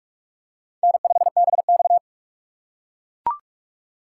Morse code, heard as a keyed tone, about 43 wpm, 700 Hz. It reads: N5BX
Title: USA Call Signs - 4 Characters - 2x Character Spacing - 45wpm